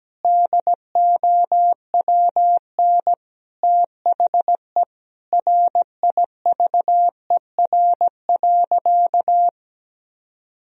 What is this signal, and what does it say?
Morse code, heard as a keyed tone, about 17 wpm, 700 Hz. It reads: DOWN THE RIVER.
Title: Sentences from Top 1000 Words - Review Part 2 - 17wpm